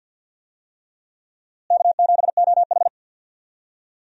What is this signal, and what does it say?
Morse code, heard as a keyed tone, about 51 wpm, 700 Hz. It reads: K7OH
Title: USA Call Signs - 4 Characters - Repeated 3x - 50wpm